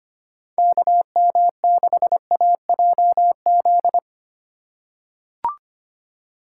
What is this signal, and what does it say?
Morse code, heard as a keyed tone, about 25 wpm, 700 Hz. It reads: KM6AJZ
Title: USA Call Signs - 6 Characters - 3x Character Spacing - 25wpm